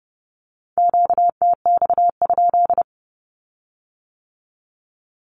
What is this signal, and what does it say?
Morse code, heard as a keyed tone, about 30 wpm, 700 Hz. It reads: QTX?
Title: Q-Codes with Spoken Meaning - 30wpm